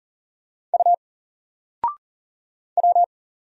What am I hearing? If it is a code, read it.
Morse code, heard as a keyed tone, about 39 wpm, 700 Hz. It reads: U W
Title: Single Letter-Number 40wpm